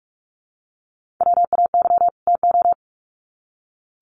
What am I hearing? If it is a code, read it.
Morse code, heard as a keyed tone, about 44 wpm, 700 Hz. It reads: WAY TO
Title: Sets of 2 Words 45wpm